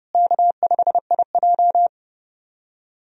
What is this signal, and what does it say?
Morse code, heard as a keyed tone, about 32 wpm, 700 Hz. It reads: K5IJ
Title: USA Call Signs - 4 Characters - Repeated 2x - 30wpm